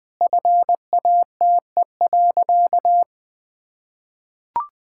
Morse code, heard as a keyed tone, about 20 wpm, 700 Hz. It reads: FATE.
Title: Sentences with 4-Word Phrases - 20wpm